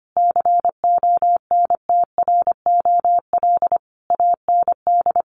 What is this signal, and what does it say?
Morse code, heard as a keyed tone, about 25 wpm, 700 Hz. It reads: CONTROL AND
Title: Sets of 5 Words Encore2 25wpm